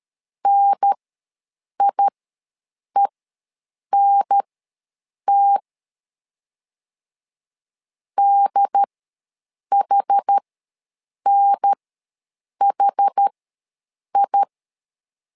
Morse code, heard as a keyed tone, about 13 wpm, spaced as for 7 wpm, 800 Hz. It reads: NIENT DHNHI
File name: LEZIONI / 2 UHDN / 2 UHDN ws 12wpm 1708